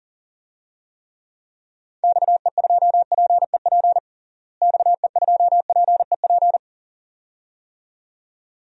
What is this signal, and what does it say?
Morse code, heard as a keyed tone, about 40 wpm, 700 Hz. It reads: XE2PEP XE2PEP
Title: DXCC-100 Call Signs - 6 Characters - Repeated 2x - 40wpm